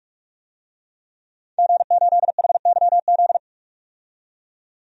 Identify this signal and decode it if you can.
Morse code, heard as a keyed tone, about 42 wpm, 700 Hz. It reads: G8HYZ
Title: DXCC-100 Call Signs - 5 Characters - 45wpm